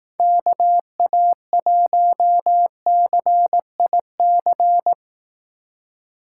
Morse code, heard as a keyed tone, about 18 wpm, 700 Hz. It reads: KA1CIC
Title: Speed Racing - USA 2x3 Call Signs 15wpm